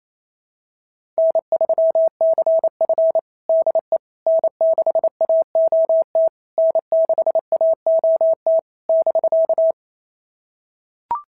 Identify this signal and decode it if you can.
Morse code, heard as a keyed tone, about 28 wpm, 650 Hz. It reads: N3CF DE N6AOT N6AOT <BK>
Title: SOTA QSO Practice 28wpm